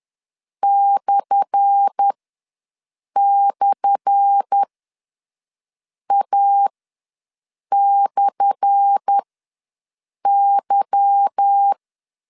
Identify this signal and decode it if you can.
Morse code, heard as a keyed tone, about 10 wpm, spaced as for 6 wpm, 800 Hz. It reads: //A/Y
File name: LEZIONI / 10 SIGNS / 10A SLASH ws 10wpm 4121